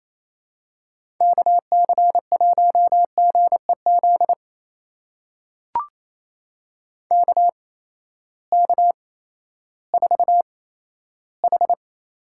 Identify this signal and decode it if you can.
Morse code, heard as a keyed tone, about 28 wpm, spaced as for 14 wpm, 700 Hz. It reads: KC1GEZ KK4H
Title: USA Call Signs - 6 Characters - 8x Character Spacing - 28wpm